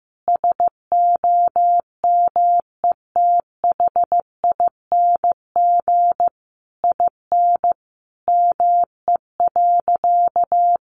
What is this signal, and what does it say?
Morse code, heard as a keyed tone, about 14 wpm, 700 Hz. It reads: SOMETHING IN ME.
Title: Sentences from Top 600 Words 15wpm